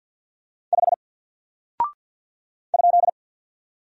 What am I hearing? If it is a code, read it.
Morse code, heard as a keyed tone, about 48 wpm, 700 Hz. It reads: 5 ?
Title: Single Character (Letter:Number:Punctuation:Prosign) Alt 50wpm